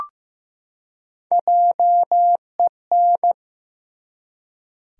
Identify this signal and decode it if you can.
Morse code, heard as a keyed tone, about 15 wpm, 700 Hz. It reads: JEN
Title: USA Common Names - 15wpm